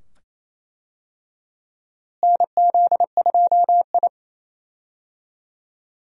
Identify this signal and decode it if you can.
Morse code, heard as a keyed tone, about 27 wpm, 700 Hz. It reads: NZ2I
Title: USA Call Signs - 4 Characters - Repeated 2x - 28wpm